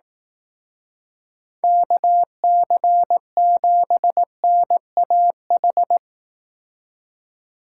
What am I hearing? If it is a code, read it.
Morse code, heard as a keyed tone, about 18 wpm, 700 Hz. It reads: KC7NAH
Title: Speed Racing - USA 2x3 Call Signs 15wpm